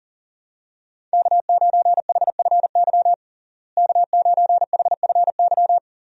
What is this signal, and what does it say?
Morse code, heard as a keyed tone, about 40 wpm, 700 Hz. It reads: K9HFY K9HFY
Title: USA Call Signs - 5 Characters - Repeated 2x - 40wpm